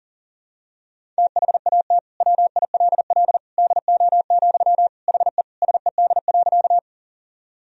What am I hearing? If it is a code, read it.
Morse code, heard as a keyed tone, about 40 wpm, 700 Hz. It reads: THAT WILL DO, HE SED.
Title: Sentences from Top 100 Words and Abbreviations 40wpm